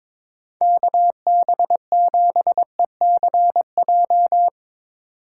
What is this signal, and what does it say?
Morse code, heard as a keyed tone, about 22 wpm, 700 Hz. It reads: KB7ECJ
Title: USA Call Signs - 6 Characters - 3x Character Spacing - 22wpm